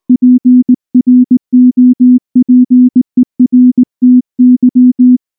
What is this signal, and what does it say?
Morse code, heard as a keyed tone, about 19 wpm, 250 Hz. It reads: PROPERTY
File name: morse / morse-training-20WPM-10